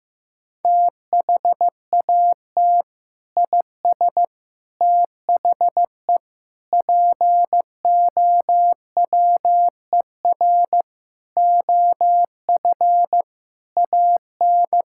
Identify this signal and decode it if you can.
Morse code, heard as a keyed tone, about 15 wpm, 700 Hz. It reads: THAT IS THE POWER OF AN
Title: Sentences from Top 1000 Words - Review Part 5 - 15wpm